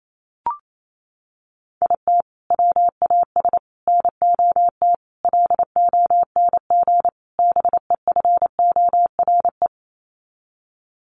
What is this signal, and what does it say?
Morse code, heard as a keyed tone, about 28 wpm, 700 Hz. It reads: IT WAS NOT LONG BEFORE
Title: Sets of 5 Words Encore2 28wpm